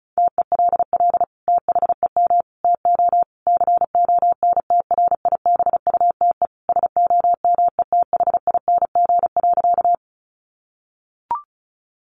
Morse code, heard as a keyed tone, about 34 wpm, 700 Hz. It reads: TELL THEM TO CONTRIBUTE SOMETHING.
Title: Sentences from Top 1400 Words 35wpm